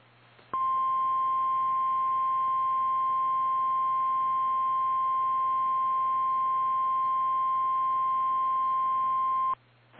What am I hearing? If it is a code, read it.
Emergency Alert System (SAME) tones (the 1050 Hz attention tone). It is heard from 0.5 to 9.6 s.
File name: eas / KKPRKXPN / KKPRKXPN